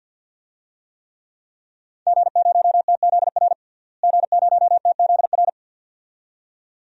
Morse code, heard as a keyed tone, about 50 wpm, 700 Hz. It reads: G0TZR G0TZR
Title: DXCC-100 Call Signs - 5 Characters - Repeated 2x - 50wpm